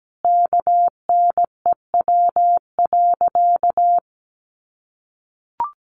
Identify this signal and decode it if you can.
Morse code, heard as a keyed tone, about 17 wpm, 700 Hz. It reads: KNEW.
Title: Sentences from Top 400 Words 17wpm